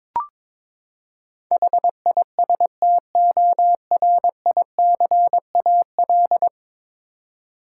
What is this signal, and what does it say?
Morse code, heard as a keyed tone, about 21 wpm, 700 Hz. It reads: HISTORICAL
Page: morse-code-ninja-practice-22wpm